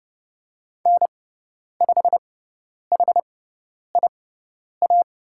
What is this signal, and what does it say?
Morse code, heard as a keyed tone, about 30 wpm, spaced as for 10 wpm, 700 Hz. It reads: N5HIA
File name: Lesson 16g - Letter c - Call Signs 30wpm